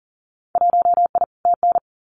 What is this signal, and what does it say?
Morse code, heard as a keyed tone, about 40 wpm, 700 Hz. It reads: 1I TN
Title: Winter Field Day Exchanges 40wpm